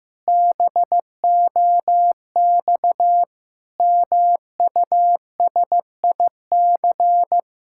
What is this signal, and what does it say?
Morse code, heard as a keyed tone, about 15 wpm, 700 Hz. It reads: BOX MUSIC
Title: Pangram Challenge v2 - 15wpm